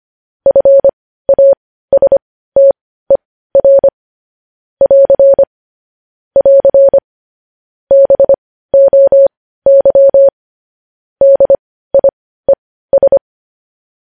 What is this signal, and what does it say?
Morse code, heard as a keyed tone, about 25 wpm, spaced as for 15 wpm, 550 Hz. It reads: FASTER <AR> <AR> BOY DIES
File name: podcast / News headlines-1x-25x15